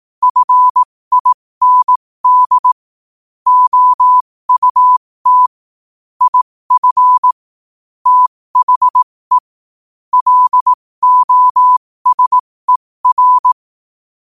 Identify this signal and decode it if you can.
Morse code, heard as a keyed tone, about 18 wpm, spaced as for 15 wpm, 1000 Hz. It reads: FIND OUT IF THE LOSER
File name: qotdfiles / 15wpm